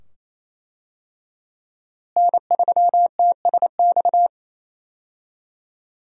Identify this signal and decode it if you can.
Morse code, heard as a keyed tone, about 27 wpm, 700 Hz. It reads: N3TSX
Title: USA Call Signs - 5 Characters - Repeated 2x - 28wpm